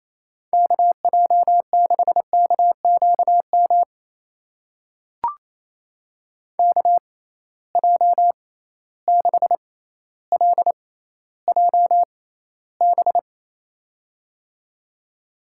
Morse code, heard as a keyed tone, about 28 wpm, spaced as for 14 wpm, 700 Hz. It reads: KJ6KQM KJ6LJB
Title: USA Call Signs - 6 Characters - 6x Character Spacing - 28wpm